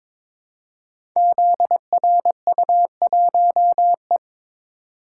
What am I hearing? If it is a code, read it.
Morse code, heard as a keyed tone, about 22 wpm, 700 Hz. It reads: ZRU1E
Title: Code Groups - 5 Letter:Numbers - 22wpm